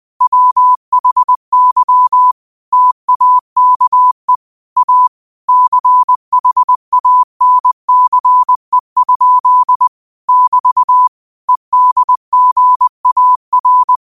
Morse code, heard as a keyed tone, about 20 wpm, 1000 Hz. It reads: WHY TAKE A CHANCE? <BT> EDGAR
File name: qotdfiles / 20wpm